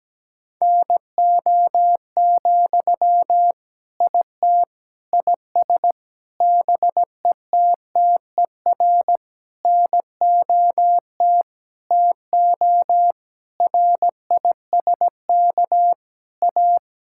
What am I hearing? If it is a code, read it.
Morse code, heard as a keyed tone, about 17 wpm, 700 Hz. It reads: NO, IT IS BETTER NOT TO RISK A
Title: Sentences from Top 1600 Words 17wpm